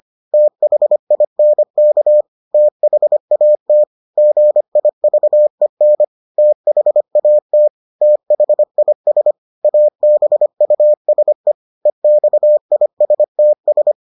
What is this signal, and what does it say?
Morse code, heard as a keyed tone, about 25 wpm, 600 Hz. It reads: THINK THAT GIVEN THAT THIS ABUSE EXISTS